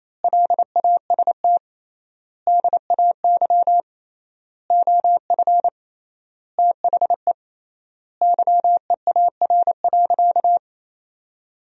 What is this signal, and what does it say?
Morse code, heard as a keyed tone, about 28 wpm, 700 Hz. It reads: LAST DAY OF THE YEAR.
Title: 3x Word-Spacing - Sentences from Top 300 Words - 28wpm